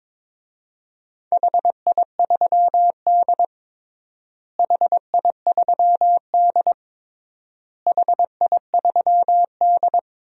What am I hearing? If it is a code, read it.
Morse code, heard as a keyed tone, about 21 wpm, 700 Hz. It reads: HI3D HI3D HI3D
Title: DXCC-100 Call Signs - 4 Characters - Repeated 3x - 22wpm